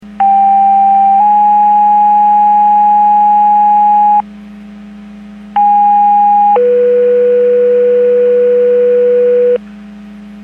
Two-tone page, heard as a one-second tone, then a three-second tone, about 790 then 830 Hz, about 820 then 490 Hz. It is heard from 0.2 to 9.6 s.
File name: Portage / 05597 06-21 Portage 154.385 Paging